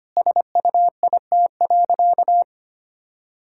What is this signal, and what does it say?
Morse code, heard as a keyed tone, about 25 wpm, 700 Hz. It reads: SUIT.
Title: Sentences from Top 1600 Words 25wpm